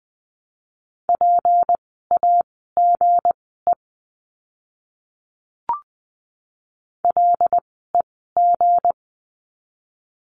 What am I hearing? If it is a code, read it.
Morse code, heard as a keyed tone, about 20 wpm, spaced as for 15 wpm, 700 Hz. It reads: PAGE LEG
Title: 160 Lesson 26f - Letter g - Words 20wpm